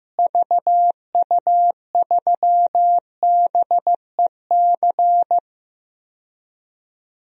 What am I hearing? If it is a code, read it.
Morse code, heard as a keyed tone, about 15 wpm, 700 Hz. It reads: VU3BEC
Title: DXCC-100 Call Signs - 6 Characters - Repeated 2x - 15wpm